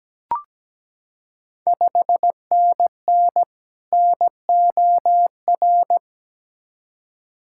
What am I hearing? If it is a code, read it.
Morse code, heard as a keyed tone, about 17 wpm, 700 Hz. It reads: 5NN NOR
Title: Ontario QSO Party Exchange 17wpm